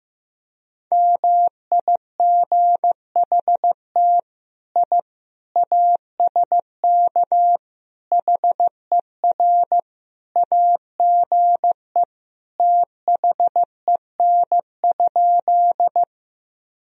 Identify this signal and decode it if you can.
Morse code, heard as a keyed tone, about 15 wpm, 700 Hz. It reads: MIGHT I ASK HER AGE THEN?
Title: Sentences from Top 1000 Words - Review Part 3 - 15wpm